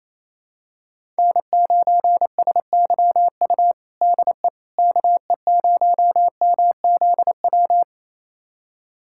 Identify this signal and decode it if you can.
Morse code, heard as a keyed tone, about 28 wpm, 700 Hz. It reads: N9SYU DE KE0MZW
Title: USA Callsign DE Callsign 28wpm